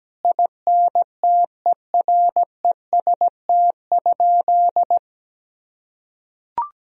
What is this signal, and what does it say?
Morse code, heard as a keyed tone, about 17 wpm, 700 Hz. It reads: INTEREST?
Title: Sentences from Top 1000 Words - Review Part 2 - 17wpm